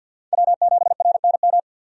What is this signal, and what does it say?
Morse code, heard as a keyed tone, about 52 wpm, 700 Hz. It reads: W7RNM
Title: USA Call Signs - 5 Characters - 50wpm